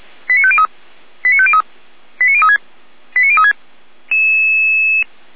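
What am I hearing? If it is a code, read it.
Five-tone selective call (ZVEI), heard as quick runs of five tones, tones of about 70 ms. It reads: 89503 89503 89036 89036
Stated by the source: Five-tone ZVEI selective call received on SDR by Feuerwehr Orlamünde.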